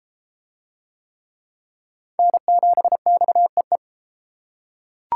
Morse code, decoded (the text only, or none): N7XEE